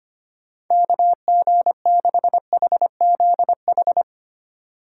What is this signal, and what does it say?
Morse code, heard as a keyed tone, about 25 wpm, 700 Hz. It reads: KG6HZH
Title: USA Call Signs - 3x Character Spacing - 25wpm